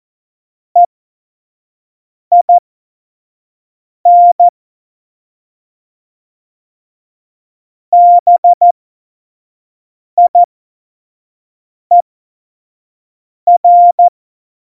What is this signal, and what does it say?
Morse code, heard as a keyed tone, about 14 wpm, spaced as for 5 wpm, 700 Hz. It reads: EIN BIER
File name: Morsekurs / Witze / Witz 12 Mai 2020, 25 F70